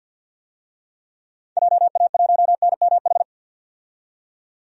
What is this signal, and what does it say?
Morse code, heard as a keyed tone, about 51 wpm, 700 Hz. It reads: JA1NMH
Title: DXCC-100 Call Signs - 6 Characters - 50wpm